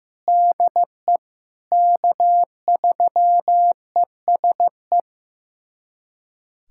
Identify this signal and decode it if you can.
Morse code, heard as a keyed tone, about 15 wpm, 700 Hz. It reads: DE K3ESE
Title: USA Callsign DE Callsign 15wpm